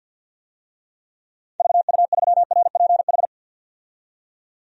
Morse code, heard as a keyed tone, about 52 wpm, 700 Hz. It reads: VU3RPH
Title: DXCC-100 Call Signs - 6 Characters - 50wpm